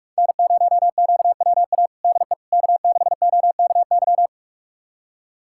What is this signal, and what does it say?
Morse code, heard as a keyed tone, about 41 wpm, 700 Hz. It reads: N0QWA DE K6OKY
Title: USA Callsign DE Callsign - 5 Characters - 45wpm